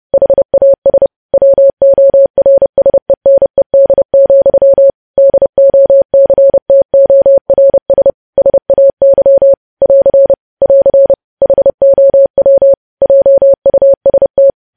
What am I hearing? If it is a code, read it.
Morse code, heard as a keyed tone, about 30 wpm, 550 Hz. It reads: HAS WORSENED, DOCTORS SAY <AR> <AR> HOW JUST
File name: podcast / News headlines-1x-30x30